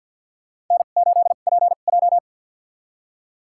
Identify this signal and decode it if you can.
Morse code, heard as a keyed tone, about 52 wpm, spaced as for 36 wpm, 700 Hz. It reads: N8PJ
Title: USA Call Signs - 4 Characters - 2x Character Spacing - 50wpm